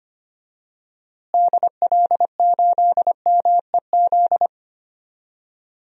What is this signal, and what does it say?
Morse code, heard as a keyed tone, about 25 wpm, 700 Hz. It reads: DL8MEZ
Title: DXCC-100 Call Signs - 6 Characters - Repeated 2x - 25wpm